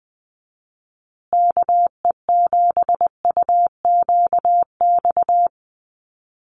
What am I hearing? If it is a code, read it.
Morse code, heard as a keyed tone, about 20 wpm, 700 Hz. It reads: KE7UQX
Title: USA Call Signs Repeated 2x 20wpm